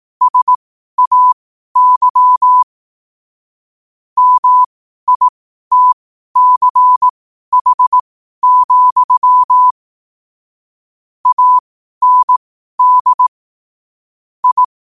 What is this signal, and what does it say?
Morse code, heard as a keyed tone, about 18 wpm, spaced as for 12 wpm, 1000 Hz. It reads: SAY MITCH, AND I
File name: qotdfiles / 12wpm